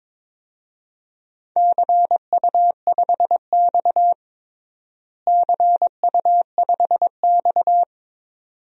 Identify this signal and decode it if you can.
Morse code, heard as a keyed tone, about 22 wpm, 700 Hz. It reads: CU5X CU5X
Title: DXCC-100 Call Signs - 4 Characters - Repeated 2x - 22wpm